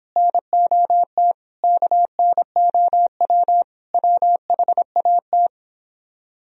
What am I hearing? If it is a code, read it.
Morse code, heard as a keyed tone, about 29 wpm, 700 Hz. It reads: NOT KNOW WHAT